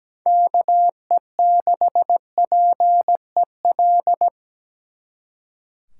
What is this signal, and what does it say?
Morse code, heard as a keyed tone, about 20 wpm, 700 Hz. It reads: KE6PEL
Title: Speed Racing - USA 2x3 Call Signs 17wpm